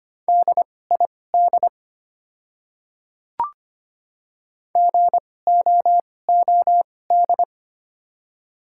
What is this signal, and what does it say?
Morse code, heard as a keyed tone, about 25 wpm, spaced as for 18 wpm, 700 Hz. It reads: DID GOOD
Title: 2x Character Spacing - Top 300 Words - 25wpm